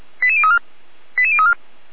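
Five-tone selective call (ZVEI), heard as quick runs of five tones, tones of about 70 ms. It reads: 88034 88034
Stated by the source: ZVEI five-tone selective call received on SDR.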